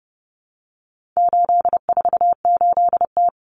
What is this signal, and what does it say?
Morse code, heard as a keyed tone, about 30 wpm, 700 Hz. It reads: 848T
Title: Common Cut Numbers - 4-Digit 30wpm